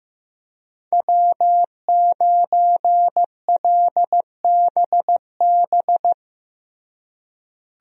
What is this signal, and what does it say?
Morse code, heard as a keyed tone, about 15 wpm, 700 Hz. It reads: W9LBB
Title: USA Call Signs - 5 Characters - Repeated 3x - 15wpm